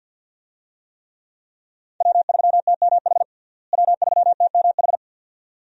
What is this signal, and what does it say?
Morse code, heard as a keyed tone, about 53 wpm, 700 Hz. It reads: W3TMH W3TMH
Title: USA Call Signs - 5 Characters - Repeated 2x - 50wpm